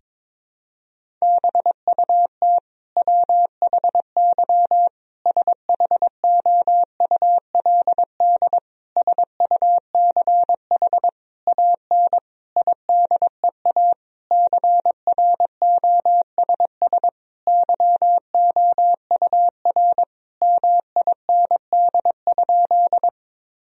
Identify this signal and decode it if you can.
Morse code, heard as a keyed tone, about 21 wpm, 700 Hz. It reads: BUT WHY SHOULD SUCH AN IDEA CROSS YOUR MIND?
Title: Sentences from Top 1000 Words - Review Part 1 - 22wpm